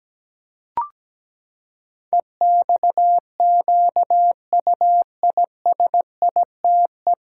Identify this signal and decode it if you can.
Morse code, heard as a keyed tone, about 17 wpm, 700 Hz. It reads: EXQUISITE
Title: Pangram Challenge v2 - 17wpm